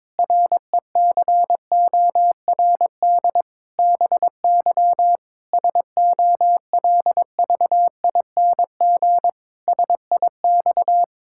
Morse code, heard as a keyed tone, about 22 wpm, 700 Hz. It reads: RECORD BY SOLVING SIX
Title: Pangram Challenge v2 - 22wpm